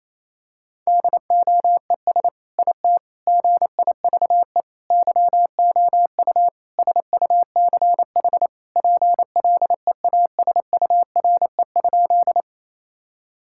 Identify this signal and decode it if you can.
Morse code, heard as a keyed tone, about 28 wpm, 700 Hz. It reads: DOES IT GIVE YOU SUCH PLEASURE?
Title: Sentences from Top 1600 Words 28wpm